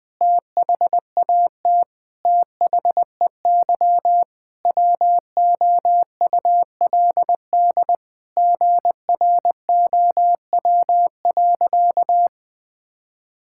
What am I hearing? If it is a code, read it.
Morse code, heard as a keyed tone, about 20 wpm, 700 Hz. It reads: THAT THEY WOULD GROW.
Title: Sentences with 9-Word Phrases - 20wpm